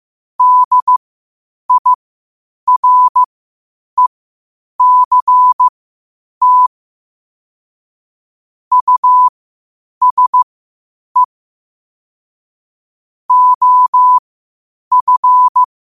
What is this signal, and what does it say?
Morse code, heard as a keyed tone, about 15 wpm, spaced as for 8 wpm, 1000 Hz. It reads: DIRECT USE OF
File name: qotdfiles / 8wpm